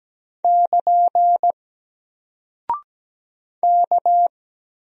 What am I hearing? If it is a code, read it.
Morse code, heard as a keyed tone, about 17 wpm, 700 Hz. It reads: <KN> K
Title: Single Character (Letter:Number:Punctuation:Prosign) Alt 17wpm